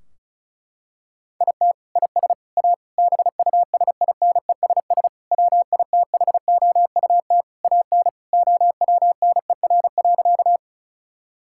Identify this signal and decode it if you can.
Morse code, heard as a keyed tone, about 34 wpm, 700 Hz. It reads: IT IS A BUSINESS WITHOUT AN OWNER.
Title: Sentences from Top 1600 Words 35wpm